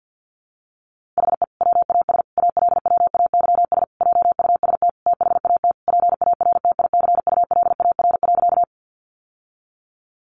Morse code, heard as a keyed tone, about 50 wpm, 700 Hz. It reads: HE WAS ALWAYS JUST THAT PARTICULAR.
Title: Sentences from Top 600 Words 50wpm